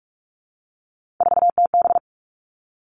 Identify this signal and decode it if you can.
Morse code, heard as a keyed tone, about 44 wpm, 700 Hz. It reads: 4TB
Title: All Cut Numbers - 3-Digit 45wpm